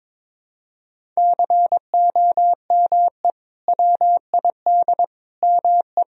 Morse code, heard as a keyed tone, about 22 wpm, 700 Hz. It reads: COME WID ME
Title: Sentences from Top 200 Words and Abbreviations 22wpm